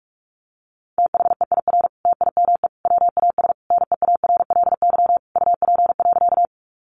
Morse code, heard as a keyed tone, about 44 wpm, 700 Hz. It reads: THEIR TIME WAS NEARLY UP.
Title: Sentences from Top 1300 Words 45wpm